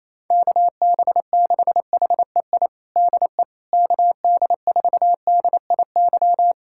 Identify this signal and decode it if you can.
Morse code, heard as a keyed tone, about 28 wpm, 700 Hz. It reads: KB6HEI DE KD4DIY
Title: USA Callsign DE Callsign - 6 Characters - 28wpm